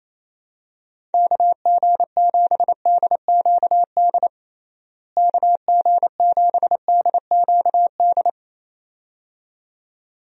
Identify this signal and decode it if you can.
Morse code, heard as a keyed tone, about 28 wpm, 700 Hz. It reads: KG7DQD KG7DQD
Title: USA Call Signs - 6 Characters - Repeated 2x - 28wpm